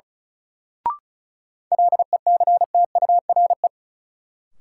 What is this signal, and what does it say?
Morse code, heard as a keyed tone, about 36 wpm, 700 Hz. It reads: LECTURE
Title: Sentences from Top 1600 Words - New Word Drill - 35wpm